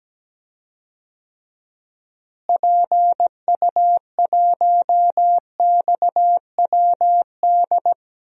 Morse code, heard as a keyed tone, about 17 wpm, 700 Hz. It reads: PU1XWD
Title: DXCC-100 Call Signs - 6 Characters - Repeated 2x - 17wpm